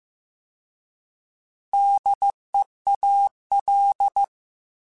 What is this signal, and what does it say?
Morse code, heard as a keyed tone, about 15 wpm, 800 Hz. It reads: DEAL
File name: Alphabetical / Alphabetical mp3s / Ds @13